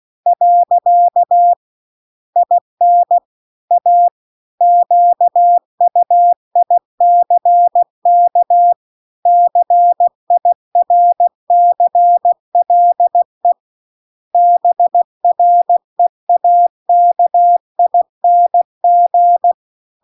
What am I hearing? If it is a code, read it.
Morse code, heard as a keyed tone, about 16 wpm, 700 Hz. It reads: . IN A QUICK CIRCLE BREAKING